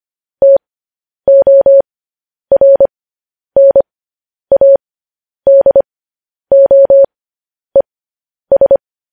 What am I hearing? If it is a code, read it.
Morse code, heard as a keyed tone, about 25 wpm, spaced as for 10 wpm, 550 Hz. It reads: TORNADOES